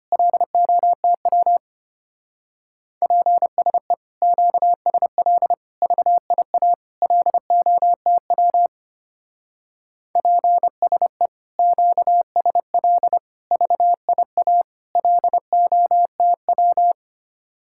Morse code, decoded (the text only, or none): LOTW PSE QSL VIA LOTW PSE QSL VIA LOTW